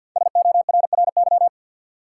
Morse code, heard as a keyed tone, about 48 wpm, 700 Hz. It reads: SORRY
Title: Five Letter Words - 50wpm